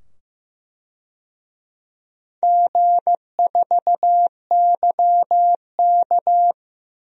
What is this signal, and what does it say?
Morse code, heard as a keyed tone, about 15 wpm, 700 Hz. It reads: G4YK
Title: DXCC-100 Call Signs - 4 Characters - 15wpm